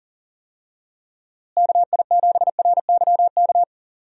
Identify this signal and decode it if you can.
Morse code, heard as a keyed tone, about 40 wpm, 700 Hz. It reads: KI7RYK
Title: USA Call Signs - 6 Characters - Repeated 2x - 40wpm